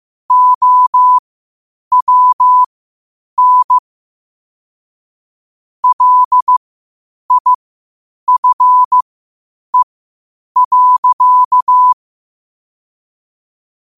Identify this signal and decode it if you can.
Morse code, heard as a keyed tone, about 15 wpm, spaced as for 8 wpm, 1000 Hz. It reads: OWN LIFE.
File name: 8wpm